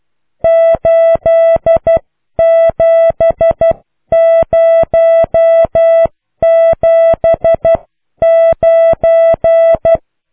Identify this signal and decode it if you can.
Morse code, heard as a keyed tone, about 12 wpm, 660 Hz. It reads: 87079